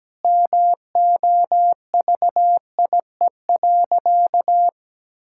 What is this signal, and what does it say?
Morse code, heard as a keyed tone, about 17 wpm, 700 Hz. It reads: MOVIE.
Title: Sentences from Top 1000 Words - Review Part 5 - 17wpm